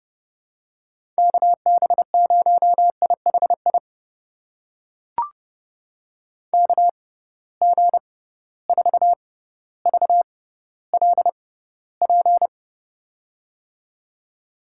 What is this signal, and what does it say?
Morse code, heard as a keyed tone, about 30 wpm, spaced as for 15 wpm, 700 Hz. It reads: KB0IHI KG4VLP
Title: USA Call Signs - 6 Characters - 6x Character Spacing - 30wpm